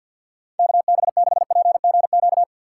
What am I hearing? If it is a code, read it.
Morse code, heard as a keyed tone, about 52 wpm, 700 Hz. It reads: K B 6 P G Q
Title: USA Call Signs - 6x Character Spacing - 50wpm